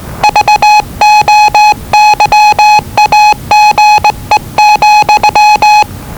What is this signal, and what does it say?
Morse code, heard as a keyed tone, about 20 wpm, 840 Hz. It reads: VOYAGE,
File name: WellingtonRadioZlwCloseDownOnHfAndMf / 01-Zlw 4277khz 417.5 qrt until 0925z